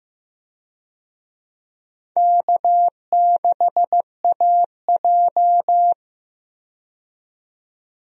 Morse code, heard as a keyed tone, about 15 wpm, 700 Hz. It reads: K6AJ
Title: USA Call Signs - 4 Characters - 15wpm